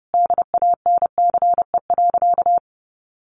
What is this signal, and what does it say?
Morse code, heard as a keyed tone, about 30 wpm, 700 Hz. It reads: DANCE.
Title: Sentences from Top 700 Words 30wpm